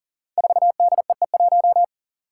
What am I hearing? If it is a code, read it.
Morse code, heard as a keyed tone, about 42 wpm, 700 Hz. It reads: 4DEE1
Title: Five Challenging Letter-Number Combinations 40wpm